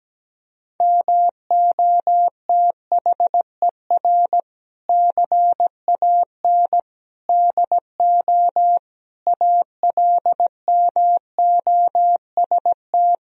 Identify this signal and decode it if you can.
Morse code, heard as a keyed tone, about 17 wpm, 700 Hz. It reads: MOTHER CAN DO ALMOST
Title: Sentences from Top 1000 Words - Review Part 4 - 17wpm